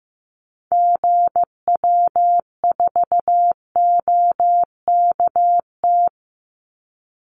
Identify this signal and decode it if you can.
Morse code, heard as a keyed tone, about 15 wpm, 700 Hz. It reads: GW4OKT
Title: Call Signs Repeated 2x - 15wpm